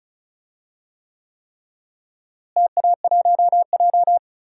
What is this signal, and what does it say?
Morse code, heard as a keyed tone, about 36 wpm, 700 Hz. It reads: TA1J
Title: DXCC-100 Call Signs - 4 Characters - Repeated 3x - 35wpm